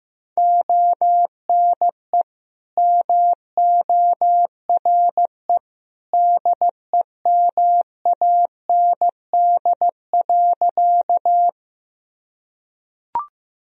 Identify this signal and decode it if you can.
Morse code, heard as a keyed tone, about 15 wpm, 700 Hz. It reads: ONE MORE DEMAND.
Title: Sentences from Top 1000 Words - Review Part 4 - 15wpm